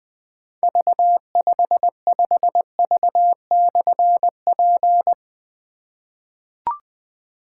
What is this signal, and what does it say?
Morse code, heard as a keyed tone, about 20 wpm, 700 Hz. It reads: V55V/P
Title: Difficult Call Signs 20wpm